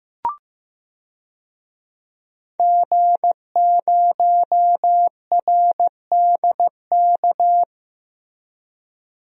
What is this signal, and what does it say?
Morse code, heard as a keyed tone, about 15 wpm, 700 Hz. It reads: G0RDK
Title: DXCC-100 Call Signs - 5 Characters - Repeated 2x - 15wpm